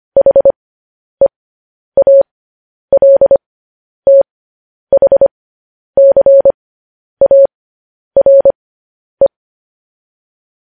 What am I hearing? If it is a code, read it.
Morse code, heard as a keyed tone, about 25 wpm, spaced as for 10 wpm, 550 Hz. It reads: HEALTHCARE
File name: podcast / News headlines-1x-25x10